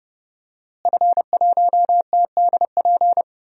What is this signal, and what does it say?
Morse code, heard as a keyed tone, about 32 wpm, 700 Hz. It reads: F1TDP
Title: DXCC-100 Call Signs - 5 Characters - Repeated 2x - 30wpm